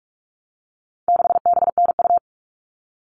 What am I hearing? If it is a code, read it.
Morse code, heard as a keyed tone, about 44 wpm, 700 Hz. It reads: 6BNU
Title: All Cut Numbers - 4-Digit 45wpm